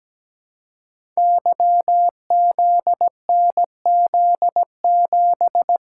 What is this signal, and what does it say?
Morse code, heard as a keyed tone, about 17 wpm, 700 Hz. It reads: YZNZ7
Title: Code Groups - 5 Letter:Numbers - 17wpm